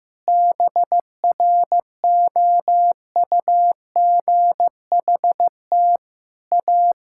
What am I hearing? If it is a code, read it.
Morse code, heard as a keyed tone, about 15 wpm, 700 Hz. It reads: BROUGHT A
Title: Sentences from Top 1000 Words - Review Part 6 - 15wpm